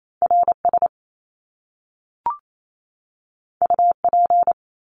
Morse code, heard as a keyed tone, about 28 wpm, 700 Hz. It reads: RS UP
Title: Top Two Letter Combinations in Words 28wpm